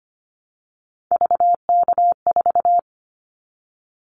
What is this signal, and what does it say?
Morse code, heard as a keyed tone, about 25 wpm, 700 Hz. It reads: VK4
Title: Callsign Prefixes for Top 50 DXCC - 25wpm